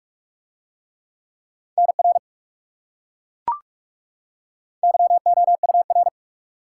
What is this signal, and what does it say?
Morse code, heard as a keyed tone, about 41 wpm, 700 Hz. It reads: NR YOUR